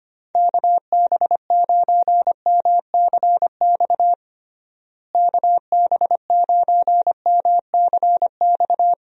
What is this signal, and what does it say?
Morse code, heard as a keyed tone, about 25 wpm, 700 Hz. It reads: KB9MCX KB9MCX
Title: USA Call Signs - 6 Characters - Repeated 3x - 25wpm